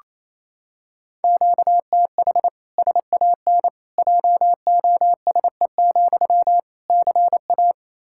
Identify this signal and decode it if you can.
Morse code, heard as a keyed tone, about 27 wpm, 700 Hz. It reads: QTH SAN JOSE, CA
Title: Practice QSO - Exchange 1 - Beginning - 28wpm